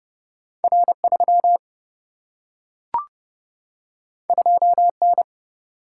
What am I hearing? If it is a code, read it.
Morse code, heard as a keyed tone, about 29 wpm, 700 Hz. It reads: R 3 2 N
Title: Single-Letter Single-Number 30wpm